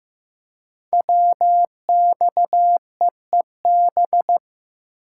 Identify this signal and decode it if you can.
Morse code, heard as a keyed tone, about 15 wpm, 700 Hz. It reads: WXEEB
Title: Code Groups - 5 Letters - 15wpm